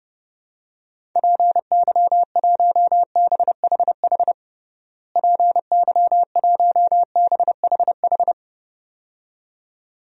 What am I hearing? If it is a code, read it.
Morse code, heard as a keyed tone, about 32 wpm, 700 Hz. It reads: PY1BHH PY1BHH
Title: DXCC-100 Call Signs - 6 Characters - Repeated 2x - 30wpm